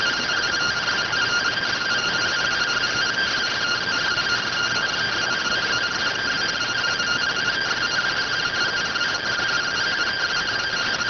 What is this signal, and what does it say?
Two-tone teleprinter signal, about 170 Hz shift.